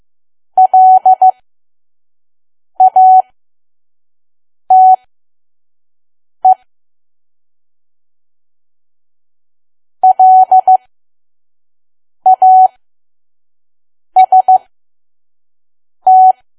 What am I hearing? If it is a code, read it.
Morse code, heard as a keyed tone, about 15 wpm, spaced as for 5 wpm, 750 Hz. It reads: LATE LAST
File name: Archive / 5 WPM files / PM